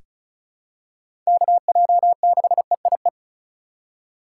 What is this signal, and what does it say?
Morse code, heard as a keyed tone, about 36 wpm, 700 Hz. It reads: KJ6EIE
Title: USA Call Signs - 6 Characters - 3x Character Spacing - 35wpm